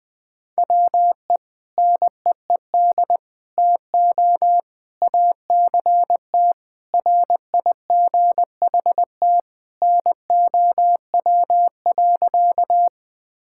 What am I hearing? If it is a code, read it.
Morse code, heard as a keyed tone, about 20 wpm, 700 Hz. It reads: WE NEED TO ACT RIGHT NOW.
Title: Sentences with 2-Word Phrases - 20wpm